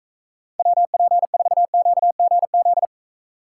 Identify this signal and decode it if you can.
Morse code, heard as a keyed tone, about 44 wpm, 700 Hz. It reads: WP4QGZ